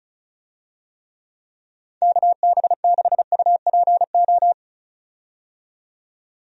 Morse code, heard as a keyed tone, about 36 wpm, 700 Hz. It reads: KB6UPO
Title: USA Call Signs - 6 Characters - 35wpm